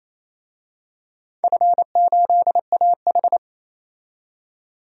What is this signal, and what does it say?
Morse code, heard as a keyed tone, about 28 wpm, 700 Hz. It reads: F8AH
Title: DXCC-100 Call Signs - 4 Characters - 28wpm